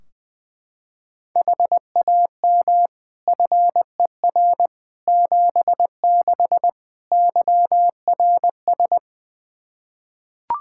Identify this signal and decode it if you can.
Morse code, heard as a keyed tone, about 20 wpm, 700 Hz. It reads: HAM FER 76 YRS
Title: Practice QSO - Exchange 3 - More Info - 20wpm